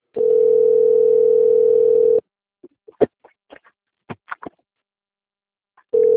Ringback tone (North America), 440 + 480 Hz.
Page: Jacarepagua - RJ